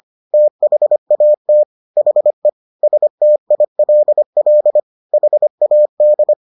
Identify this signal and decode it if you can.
Morse code, heard as a keyed tone, about 25 wpm, 600 Hz. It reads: THAT HE STILL HAD